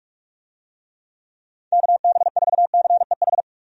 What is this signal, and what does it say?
Morse code, heard as a keyed tone, about 46 wpm, 700 Hz. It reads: KB4CEH
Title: USA Call Signs - 6 Characters - 45wpm